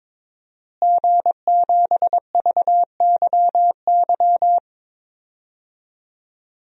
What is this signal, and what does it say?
Morse code, heard as a keyed tone, about 22 wpm, 700 Hz. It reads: G7VYY
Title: DXCC-100 Call Signs - 5 Characters - Repeated 2x - 22wpm